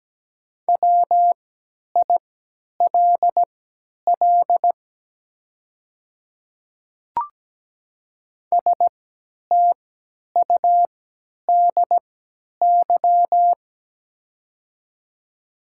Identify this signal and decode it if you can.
Morse code, heard as a keyed tone, about 17 wpm, spaced as for 10 wpm, 700 Hz. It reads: WILL STUDY
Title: 3x Character Spacing - Top 500 Words - 17wpm